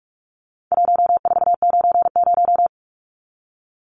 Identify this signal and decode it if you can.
Morse code, heard as a keyed tone, about 44 wpm, 700 Hz. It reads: 1490
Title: Numbers - 4-Digit 45wpm